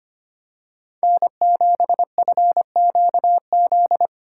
Morse code, heard as a keyed tone, about 25 wpm, 700 Hz. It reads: N7FQZ
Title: USA Call Signs - 2x Character Spacing - 25wpm